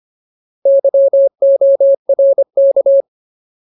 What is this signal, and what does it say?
Morse code, heard as a keyed tone, about 25 wpm, 550 Hz. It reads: YORK
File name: Ultimate Word List / 25WPM Ultimate